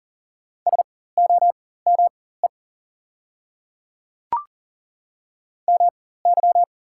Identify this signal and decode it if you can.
Morse code, heard as a keyed tone, about 37 wpm, spaced as for 18 wpm, 700 Hz. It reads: SOME MY
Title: 4x Character Spacing - Top 200 Words - 40wpm